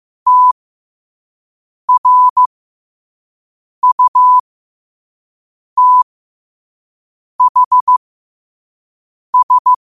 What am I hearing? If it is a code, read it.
Morse code, heard as a keyed tone, about 15 wpm, spaced as for 5 wpm, 1000 Hz. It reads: TRUTHS
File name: qotdfiles / 5wpm